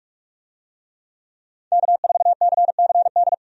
Morse code, heard as a keyed tone, about 45 wpm, 700 Hz. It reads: K4CCD
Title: USA Call Signs - 5 Characters - 45wpm